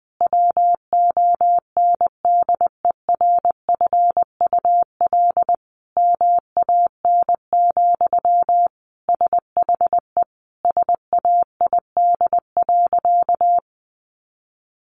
Morse code, heard as a keyed tone, about 20 wpm, 700 Hz. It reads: WONDERFUL MAN, SHE SAID.
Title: Sentences from Top 1100 Words 20wpm